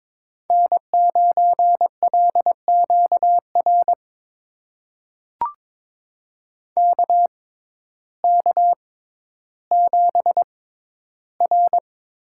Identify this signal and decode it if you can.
Morse code, heard as a keyed tone, about 22 wpm, spaced as for 12 wpm, 700 Hz. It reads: N9LQR KK7R
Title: USA Call Signs - 5 Characters - 6x Character Spacing - 22wpm